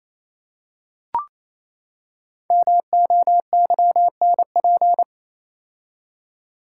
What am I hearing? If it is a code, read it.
Morse code, heard as a keyed tone, about 28 wpm, 700 Hz. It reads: MOYNP
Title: Code Groups - 5 Letter:Numbers - 28wpm